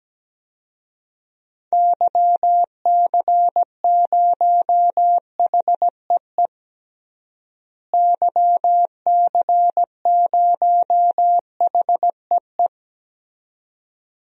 Morse code, heard as a keyed tone, about 17 wpm, 700 Hz. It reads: YC0HEE YC0HEE
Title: DXCC-100 Call Signs - 6 Characters - Repeated 2x - 17wpm